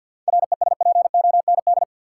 Morse code, heard as a keyed tone, about 50 wpm, 700 Hz. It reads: RESPOND
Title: Seven Letter Words - 50wpm